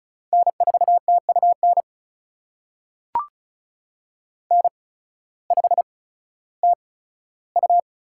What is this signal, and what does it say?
Morse code, heard as a keyed tone, about 34 wpm, spaced as for 16 wpm, 700 Hz. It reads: N4TUN N5TU
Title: USA Call Signs - 5 Characters - 8x Character Spacing - 35wpm